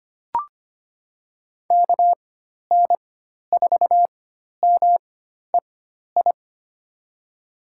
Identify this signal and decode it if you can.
Morse code, heard as a keyed tone, about 25 wpm, spaced as for 12 wpm, 700 Hz. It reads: KN4MEI
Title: USA Call Signs - 6 Characters - 4x Character Spacing - 25wpm